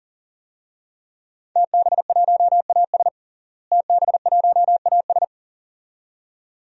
Morse code, heard as a keyed tone, about 37 wpm, 700 Hz. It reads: TB1AS TB1AS
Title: DXCC-100 Call Signs - 5 Characters - Repeated 2x - 40wpm